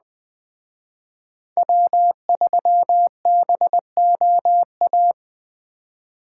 Morse code, heard as a keyed tone, about 20 wpm, 700 Hz. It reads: W3BOA
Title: Speed Racing - USA 1x3 Call Signs 17wpm